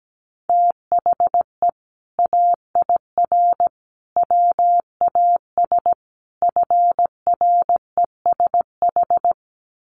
Morse code, heard as a keyed tone, about 17 wpm, 700 Hz. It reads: THE AIR WAS FRESH
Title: Sentences from Top 1200 Words 17wpm